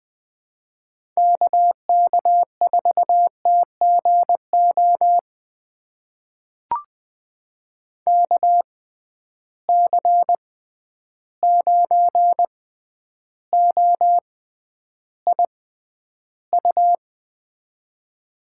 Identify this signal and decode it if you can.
Morse code, heard as a keyed tone, about 20 wpm, spaced as for 10 wpm, 700 Hz. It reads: KK4TGO KC9OIU
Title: USA Call Signs - 6 Characters - 6x Character Spacing - 20wpm